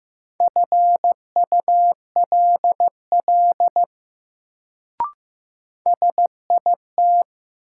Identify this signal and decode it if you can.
Morse code, heard as a keyed tone, about 15 wpm, 700 Hz. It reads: FULL SIT
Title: Sentences from Top 500 Words - New Word Drill - 15wpm